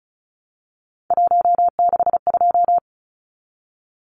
Morse code, heard as a keyed tone, about 34 wpm, 700 Hz. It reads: 162
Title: Numbers - 3-Digit 35wpm